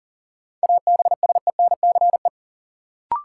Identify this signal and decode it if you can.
Morse code, heard as a keyed tone, about 40 wpm, 700 Hz. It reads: ABSENCE
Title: Sets of 4 Words - New Word Drill - 40wpm